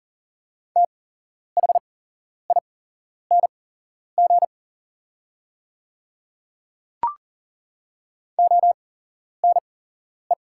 Morse code, heard as a keyed tone, about 38 wpm, spaced as for 11 wpm, 700 Hz. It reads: THING ONE
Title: 8x Character Spacing - Top 100 Words - 40wpm